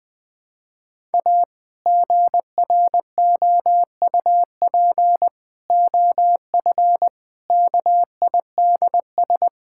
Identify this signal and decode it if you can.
Morse code, heard as a keyed tone, about 20 wpm, 700 Hz. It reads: A GROUP OF KIDS
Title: Sentences with 3-Word Phrases - 20wpm